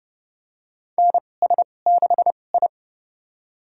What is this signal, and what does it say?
Morse code, heard as a keyed tone, about 30 wpm, spaced as for 22 wpm, 700 Hz. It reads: NS6I
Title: USA Call Signs - 4 Characters - 2x Character Spacing - 30wpm